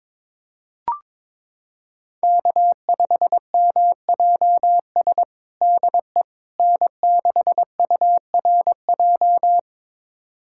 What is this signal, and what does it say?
Morse code, heard as a keyed tone, about 22 wpm, 700 Hz. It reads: K5MJS DE N6URJ
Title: USA Callsign DE Callsign - 5 Characters - 22wpm